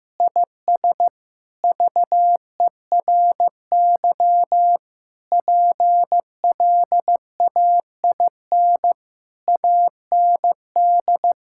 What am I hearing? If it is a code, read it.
Morse code, heard as a keyed tone, about 15 wpm, 700 Hz. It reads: IS VERY PLAIN AND
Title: Sentences from Top 1000 Words - Review Part 5 - 15wpm